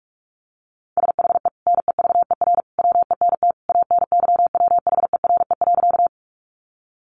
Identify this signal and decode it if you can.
Morse code, heard as a keyed tone, about 44 wpm, 700 Hz. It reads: SHE NEVER WENT ANYWHERE.
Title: Sentences from Top 1400 Words 45wpm